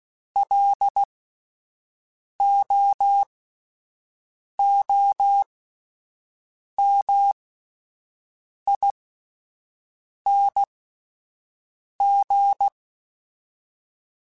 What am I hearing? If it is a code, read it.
Morse code, heard as a keyed tone, about 16 wpm, spaced as for 5 wpm, 800 Hz. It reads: LOOMING
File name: Alphabetical / Alphabetical mp3s / Ls @5